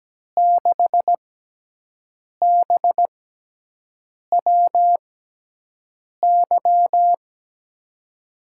Morse code, heard as a keyed tone, about 17 wpm, spaced as for 6 wpm, 700 Hz. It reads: 6BWY
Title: USA Call Signs - 6 Characters - 6x Character Spacing - 17wpm